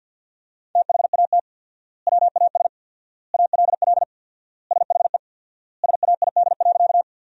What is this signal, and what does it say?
Morse code, heard as a keyed tone, about 50 wpm, 700 Hz. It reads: THAT WAS ALL SHE SAID.
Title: 4x Word-Spacing - Sentences from Top 100 Words - 50wpm